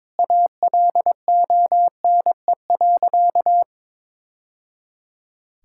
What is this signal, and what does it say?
Morse code, heard as a keyed tone, about 22 wpm, 700 Hz. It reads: ALONE.
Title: Sentences from Top 1000 Words - Review Part 5 - 22wpm